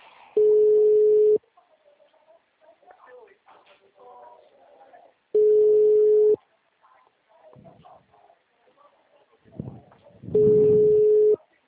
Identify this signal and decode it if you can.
Ringback tone (Europe, 425 Hz), with speech.
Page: TLMK - Rio de janeiro 2